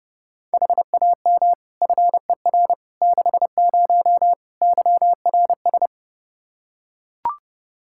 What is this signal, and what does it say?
Morse code, heard as a keyed tone, about 30 wpm, 700 Hz. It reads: HAM FER 60 YRS
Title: Practice QSO - Exchange 3 - More Info - 30wpm